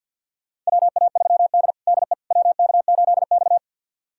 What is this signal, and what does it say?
Morse code, heard as a keyed tone, about 48 wpm, 690 Hz. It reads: WA3D DE WK8X
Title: USA Callsign DE Callsign - 4 Characters - 50wpm